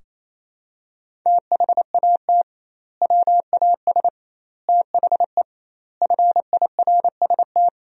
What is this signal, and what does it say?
Morse code, heard as a keyed tone, about 27 wpm, 700 Hz. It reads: THAT WAS THE FIRST
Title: 2x Word-Spacing - Sentences from Top 300 Words - 28wpm